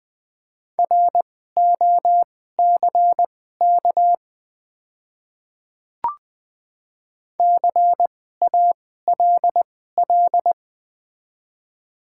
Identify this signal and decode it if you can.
Morse code, heard as a keyed tone, about 20 wpm, spaced as for 15 wpm, 690 Hz. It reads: ROCK CALL
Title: 2x Character Spacing - Top 500 Words - 20wpm